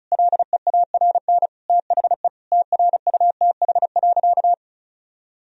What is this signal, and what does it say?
Morse code, heard as a keyed tone, about 34 wpm, 700 Hz. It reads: LEARN THE TRUTH.
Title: Sentences from Top 1000 Words - Review Part 4 - 35wpm